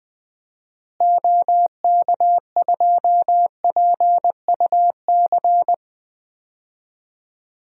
Morse code, heard as a keyed tone, about 20 wpm, 700 Hz. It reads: OK2PUC
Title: DXCC-100 Call Signs - 6 Characters - Repeated 3x - 20wpm